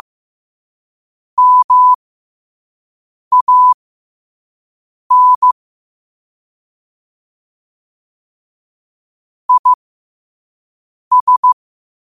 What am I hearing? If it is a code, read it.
Morse code, heard as a keyed tone, about 15 wpm, spaced as for 5 wpm, 1000 Hz. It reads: MAN IS
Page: qotd-at-5-wpm